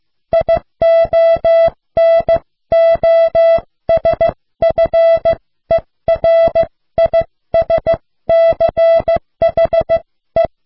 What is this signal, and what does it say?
Morse code, heard as a keyed tone, about 15 wpm, spaced as for 13 wpm, 660 Hz. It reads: IONOSFERISCHE